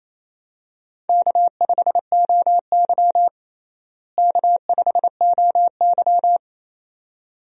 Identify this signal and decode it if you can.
Morse code, heard as a keyed tone, about 28 wpm, 700 Hz. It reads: K5OY K5OY
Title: USA Call Signs - 4 Characters - Repeated 2x - 28wpm